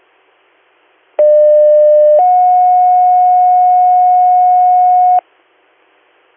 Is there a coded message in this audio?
Yes: two-tone page from 1.2 to 5.2 s, heard as a one-second tone, then a three-second tone, about 600 then 730 Hz.